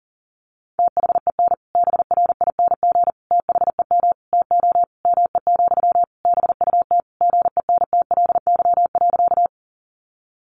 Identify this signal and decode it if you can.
Morse code, heard as a keyed tone, about 40 wpm, 700 Hz. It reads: THEN BRING THEM TO ME, BUT GENTLY.
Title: Sentences from Top 1500 Words 40wpm